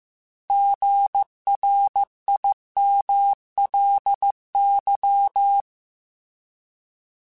Morse code, heard as a keyed tone, about 15 wpm, 800 Hz. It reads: GRIMLY